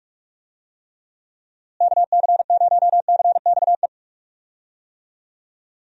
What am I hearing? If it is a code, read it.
Morse code, heard as a keyed tone, about 46 wpm, 700 Hz. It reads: KC0CXE
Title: USA Call Signs - 6 Characters - Repeated 2x - 45wpm